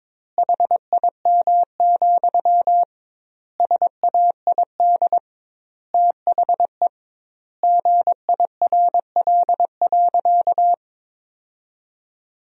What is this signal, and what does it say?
Morse code, heard as a keyed tone, about 22 wpm, 700 Hz. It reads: HIM, SAID THE GIRL.
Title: 2x Word-Spacing - Sentences from Top 400 Words - 22wpm